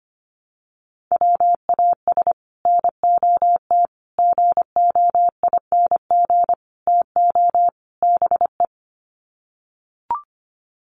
Morse code, heard as a keyed tone, about 25 wpm, 700 Hz. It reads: WAS NOT GOING TO BE
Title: Sets of 5 Words Encore 25wpm